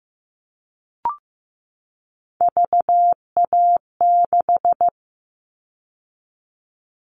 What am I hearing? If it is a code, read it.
Morse code, heard as a keyed tone, about 15 wpm, 700 Hz. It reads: VA6
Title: Callsign Prefixes for Top 50 DXCC - 15wpm